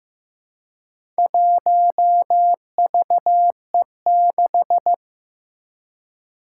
Morse code, heard as a keyed tone, about 15 wpm, 700 Hz. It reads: 1VE6
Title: Four Challenging Letter-Number Combinations 15wpm